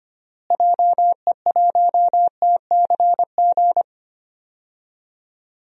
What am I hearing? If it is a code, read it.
Morse code, heard as a keyed tone, about 25 wpm, 700 Hz. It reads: JE1TCG
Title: DXCC-100 Call Signs - 6 Characters - Repeated 3x - 25wpm